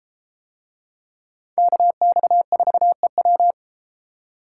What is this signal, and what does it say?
Morse code, heard as a keyed tone, about 32 wpm, 700 Hz. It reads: KX4EW